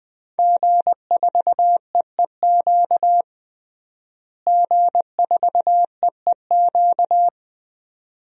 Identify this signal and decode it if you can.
Morse code, heard as a keyed tone, about 20 wpm, 700 Hz. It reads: G4EEQ G4EEQ
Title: DXCC-100 Call Signs - 5 Characters - Repeated 2x - 20wpm